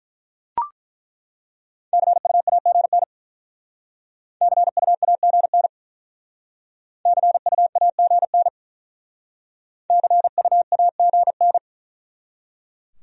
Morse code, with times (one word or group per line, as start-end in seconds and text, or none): 1.9-3.0 CUAGN
4.4-5.7 CUAGN
7.1-8.5 CUAGN
9.9-11.6 CUAGN